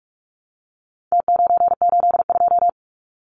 Morse code, heard as a keyed tone, about 44 wpm, 700 Hz. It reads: T982
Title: Numbers - Contest Style - 4-Digit 45wpm